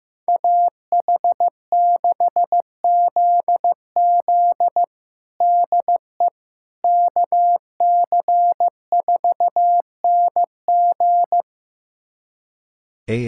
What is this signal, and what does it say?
Morse code, heard as a keyed tone, about 15 wpm, 700 Hz. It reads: AH6ZZ DE KC4NG
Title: USA Callsign DE Callsign - 5 Characters - 15wpm